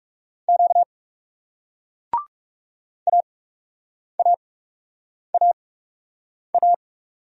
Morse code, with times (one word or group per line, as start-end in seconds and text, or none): 0.5-0.8 Q
3.1-6.7 AAAA